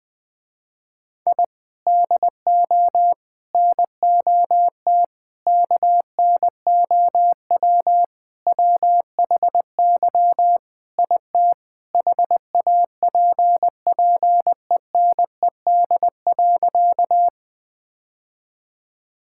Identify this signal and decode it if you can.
Morse code, heard as a keyed tone, about 20 wpm, 700 Hz. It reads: I DO NOT KNOW WHY IT HAPPENED.
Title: Sentences with 5-Word Phrases - Spoken Beginning - 20wpm